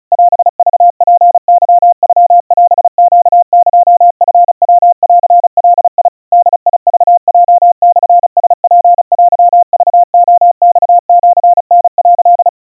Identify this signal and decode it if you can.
Morse code (Japanese Wabun code), heard as a keyed tone, about 34 wpm, 700 Hz. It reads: オクツケノオネエチヤンガ ボクヲモラツテクレマシタ」
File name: kawa / Morse sample / kamyu 130 cpm 700 Hz Crispy